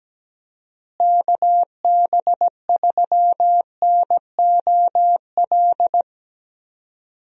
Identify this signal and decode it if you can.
Morse code, heard as a keyed tone, about 17 wpm, 700 Hz. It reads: KB3NOL
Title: USA Call Signs - 6 Characters - Repeated 2x - 17wpm